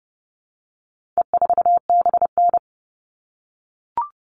Morse code, heard as a keyed tone, about 30 wpm, 700 Hz. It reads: E4BN
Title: All Cut Numbers - 4-Digit 30wpm